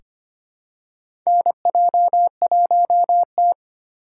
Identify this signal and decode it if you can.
Morse code, heard as a keyed tone, about 25 wpm, 700 Hz. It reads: NJ1T
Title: K1USN SST QSO Practice - 25wpm